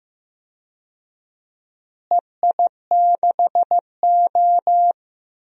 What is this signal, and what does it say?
Morse code, heard as a keyed tone, about 15 wpm, 700 Hz. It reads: EI6O
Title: DXCC-100 Call Signs - 4 Characters - Repeated 2x - 15wpm